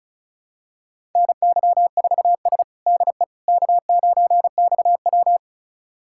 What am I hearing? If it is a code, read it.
Morse code, heard as a keyed tone, about 36 wpm, 700 Hz. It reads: NY4S DE K9XW
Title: USA Callsign DE Callsign - 4 Characters - 35wpm